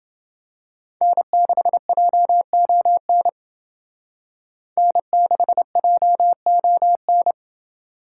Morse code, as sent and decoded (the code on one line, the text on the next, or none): -. -.... .--- --- -. / -. -.... .--- --- -.
N6JON N6JON